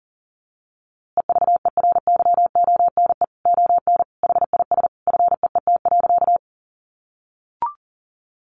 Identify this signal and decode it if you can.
Morse code, heard as a keyed tone, about 40 wpm, 700 Hz. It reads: EVERYONE ON HIS FEET.
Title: A Sound Of Thunder 40wpm